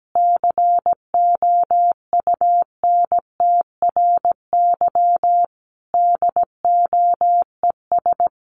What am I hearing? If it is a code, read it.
Morse code, heard as a keyed tone, about 17 wpm, 700 Hz. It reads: COUNTRY DOES
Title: Sentences from Top 300 Words 17wpm